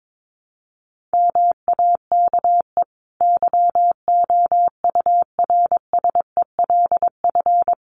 Morse code, heard as a keyed tone, about 22 wpm, 700 Hz. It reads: MAKE YOURSELF
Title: Sentences from Top 600 Words 22wpm